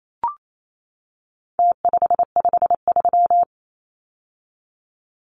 Morse code, heard as a keyed tone, about 28 wpm, 700 Hz. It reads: T553
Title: Numbers - Contest Style - 4-Digit 28wpm